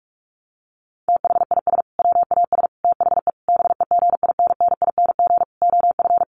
Morse code, heard as a keyed tone, about 44 wpm, 700 Hz. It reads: THIS WAS THE BEGINNING OF
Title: Sentences from Top 900 Words 45wpm